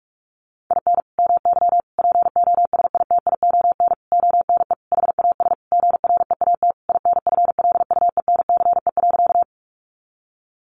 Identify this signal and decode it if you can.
Morse code, heard as a keyed tone, about 44 wpm, 700 Hz. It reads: IN MY POSITION ONE HAS GREAT INFLUENCE.
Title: Sentences from Top 1100 Words 45wpm